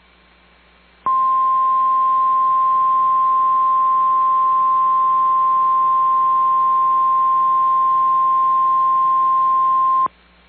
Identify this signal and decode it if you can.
Emergency Alert System (SAME) tones (the 1050 Hz attention tone). It is heard from 1.0 to 10.1 s.
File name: EAS Logs / KMNR